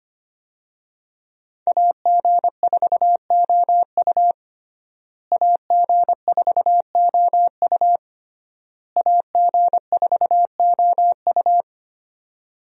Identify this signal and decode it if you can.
Morse code, heard as a keyed tone, about 25 wpm, 700 Hz. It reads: AG4OU AG4OU AG4OU
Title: USA Call Signs - 5 Characters - Repeated 3x - 25wpm